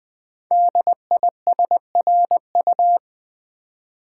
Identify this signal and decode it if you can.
Morse code, heard as a keyed tone, about 20 wpm, 700 Hz. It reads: DISRU
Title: Build A Word - 5 Letters At A Time - 10-Letter Words - 20wpm